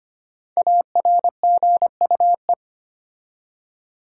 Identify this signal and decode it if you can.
Morse code, heard as a keyed tone, about 25 wpm, 700 Hz. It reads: ARGUE
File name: 5 Letter Words - First 3 Letters Given - 25wpm